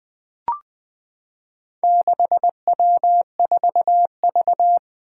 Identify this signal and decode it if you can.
Morse code, heard as a keyed tone, about 20 wpm, 700 Hz. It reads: 6W4V
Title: Four Challenging Letter-Number Combinations 20wpm